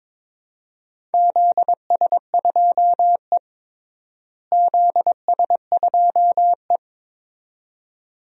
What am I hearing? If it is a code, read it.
Morse code, heard as a keyed tone, about 22 wpm, 700 Hz. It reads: ZS2E ZS2E
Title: DXCC-100 Call Signs - 4 Characters - Repeated 2x - 22wpm